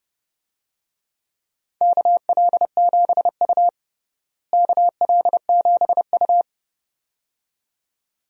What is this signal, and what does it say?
Morse code, heard as a keyed tone, about 30 wpm, 700 Hz. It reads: KL7U KL7U
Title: DXCC-100 Call Signs - 4 Characters - Repeated 2x - 30wpm